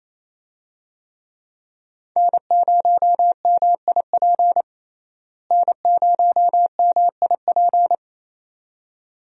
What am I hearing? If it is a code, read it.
Morse code, heard as a keyed tone, about 28 wpm, 700 Hz. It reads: N0MIP N0MIP
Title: USA Call Signs - 5 Characters - Repeated 2x - 28wpm